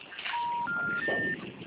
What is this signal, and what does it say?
Special information tone, with noise.